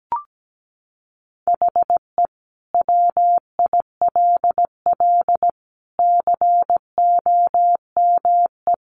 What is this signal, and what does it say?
Morse code, heard as a keyed tone, about 17 wpm, 700 Hz. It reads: HE WILL COME
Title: Sentences from Top 1400 Words 17wpm